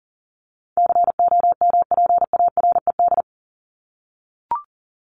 Morse code, heard as a keyed tone, about 40 wpm, 700 Hz. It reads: COMPARED
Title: ED Suffix - Top 100 Words with Base Words 40wpm